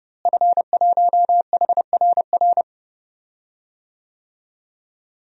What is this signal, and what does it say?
Morse code, heard as a keyed tone, about 32 wpm, 700 Hz. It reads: F1HRR
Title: DXCC-100 Call Signs - 5 Characters - Repeated 3x - 30wpm